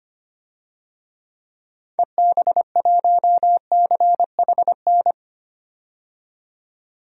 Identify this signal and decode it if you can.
Morse code, heard as a keyed tone, about 25 wpm, 700 Hz. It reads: EB1CHN
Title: DXCC-100 Call Signs - 6 Characters - 25wpm